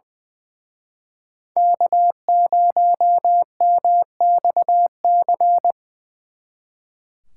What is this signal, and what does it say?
Morse code, heard as a keyed tone, about 23 wpm, 700 Hz. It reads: K0MXC
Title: Speed Racing - USA 1x3 Call Signs 20wpm